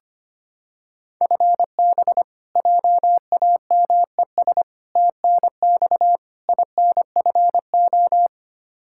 Morse code, heard as a keyed tone, about 25 wpm, 700 Hz. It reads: FB JAMES TNX INFO
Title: Practice QSO - Exchange 3 - More Info - 25wpm